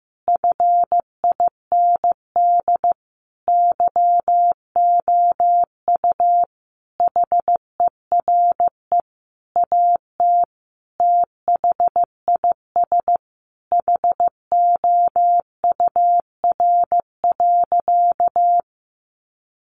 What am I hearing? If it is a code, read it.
Morse code, heard as a keyed tone, about 15 wpm, 700 Hz. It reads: FIND YOU HERE AT THIS HOUR.
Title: Sentences from Top 1400 Words 15wpm